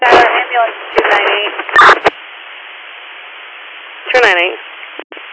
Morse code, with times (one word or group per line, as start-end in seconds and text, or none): none